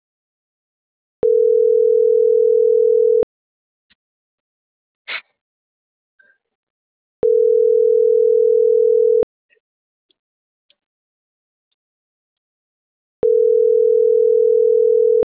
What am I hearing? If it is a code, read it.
Ringback tone (North America), 440 + 480 Hz.